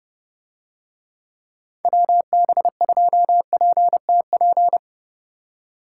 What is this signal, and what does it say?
Morse code, heard as a keyed tone, about 28 wpm, 700 Hz. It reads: WB2PTP